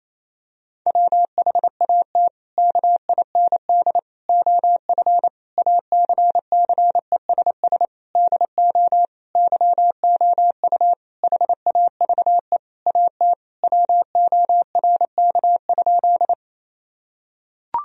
Morse code, heard as a keyed tone, about 28 wpm, 700 Hz. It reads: WHAT KIND OF ACCESS DO YOU HAVE AT WORK?
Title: Sentences from Top 1000 Words - Review Part 4 - 28wpm